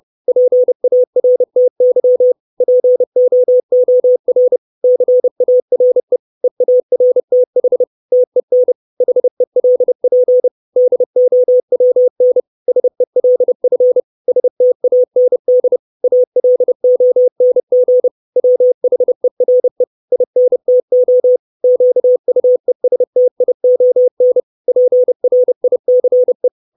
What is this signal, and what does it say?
Morse code, heard as a keyed tone, about 30 wpm, 500 Hz. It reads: PARTY POOR CARE EARTH TEN HELP DOWN SELF STAND ALONG WHERE INTO QUESTION PRICE
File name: 632 Words 30WPM